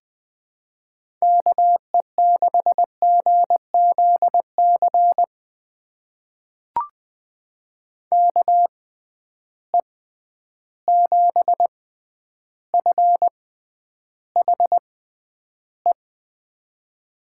Morse code, heard as a keyed tone, about 20 wpm, spaced as for 10 wpm, 700 Hz. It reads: KE6GZC KE7FHE
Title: USA Call Signs - 6 Characters - 6x Character Spacing - 20wpm